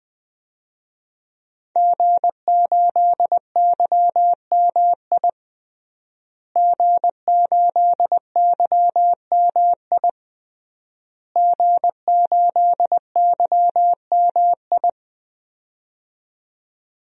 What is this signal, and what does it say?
Morse code, heard as a keyed tone, about 20 wpm, 700 Hz. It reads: G8YMI G8YMI G8YMI
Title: DXCC-100 Call Signs - 5 Characters - Repeated 3x - 20wpm